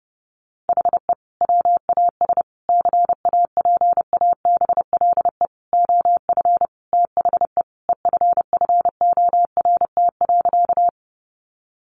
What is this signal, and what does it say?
Morse code, heard as a keyed tone, about 30 wpm, 700 Hz. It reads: HE WAS CAPABLE OF THE EFFORT.
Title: Sentences from Top 1200 Words 30wpm